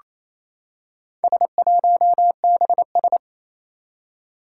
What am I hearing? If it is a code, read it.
Morse code, heard as a keyed tone, about 28 wpm, 700 Hz. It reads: S1BS
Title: Four Challenging Letter-Number Combinations 28wpm